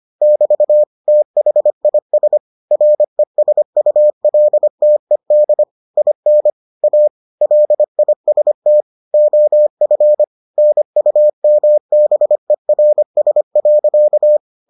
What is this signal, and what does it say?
Morse code, heard as a keyed tone, about 25 wpm, 600 Hz. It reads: <BT> THIS RESULTED IN A LIST OF NUMBERS.